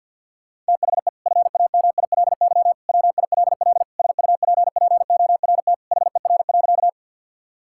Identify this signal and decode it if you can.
Morse code, heard as a keyed tone, about 50 wpm, 700 Hz. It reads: THE FAMILY WILL SUPPORT HER.
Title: Sentences from Top 1000 Words - Review Part 5 - 50wpm